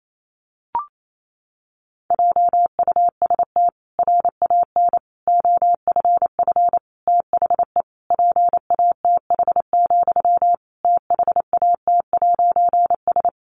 Morse code, heard as a keyed tone, about 28 wpm, 700 Hz. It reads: JUST RAN OFF THE PATH, THAT'S
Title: A Sound Of Thunder 28wpm